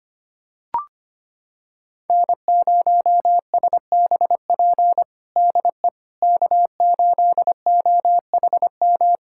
Morse code, heard as a keyed tone, about 25 wpm, 700 Hz. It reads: N0SBP DE K8OHM
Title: USA Callsign DE Callsign - 5 Characters - 25wpm